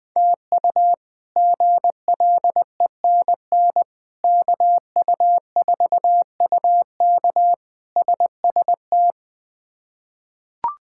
Morse code, heard as a keyed tone, about 20 wpm, 700 Hz. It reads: TU GLENN KU4UK SST
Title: K1USN SST QSO Practice - 20wpm